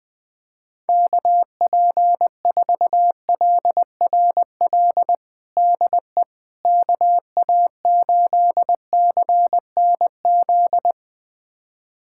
Morse code, heard as a keyed tone, about 20 wpm, 700 Hz. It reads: KP4LRL DE KA8CNZ
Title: USA Callsign DE Callsign 20wpm